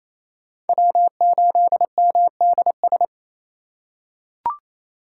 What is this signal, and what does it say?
Morse code, heard as a keyed tone, about 28 wpm, 700 Hz. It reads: W8MDS
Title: USA Call Signs - 5 Characters - 4x Character Spacing - 28wpm